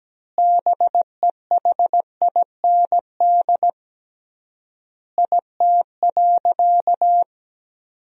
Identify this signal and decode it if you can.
Morse code, heard as a keyed tone, about 17 wpm, 700 Hz. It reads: BEHIND IT.
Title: 3x Word-Spacing - Sentences from Top 500 Words - 17wpm